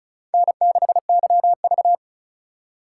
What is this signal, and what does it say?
Morse code, heard as a keyed tone, about 34 wpm, 700 Hz. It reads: N6YV
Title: USA Call Signs - 4 Characters - 3x Character Spacing - 35wpm